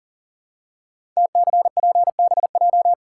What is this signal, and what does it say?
Morse code, heard as a keyed tone, about 41 wpm, 690 Hz. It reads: TCPBJ
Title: Code Groups - 5 Letters - 40wpm